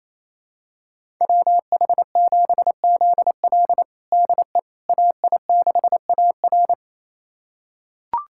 Morse code, heard as a keyed tone, about 27 wpm, 700 Hz. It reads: WH7ZL DE AI6AR
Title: USA Callsign DE Callsign - 5 Characters - 28wpm